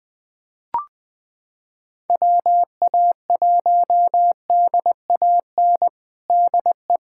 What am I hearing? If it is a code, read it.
Morse code, heard as a keyed tone, about 20 wpm, 700 Hz. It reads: WA1DAN DE
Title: USA Callsign DE Callsign 20wpm